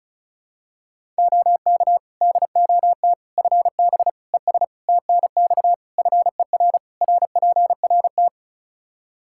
Morse code, heard as a keyed tone, about 34 wpm, 700 Hz. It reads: OK DOT FB ES TNX FER RPRT
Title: Practice QSO - Exchange 2 - Info - 35wpm